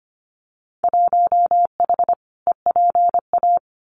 Morse code, heard as a keyed tone, about 25 wpm, 700 Hz. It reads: 1H EPA
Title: Winter Field Day Exchanges 25wpm